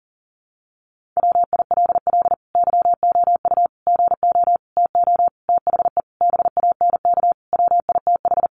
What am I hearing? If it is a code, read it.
Morse code, heard as a keyed tone, about 40 wpm, 700 Hz. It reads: WILL YOU GO TO THE BANK WITH
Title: Sentences from Top 900 Words 40wpm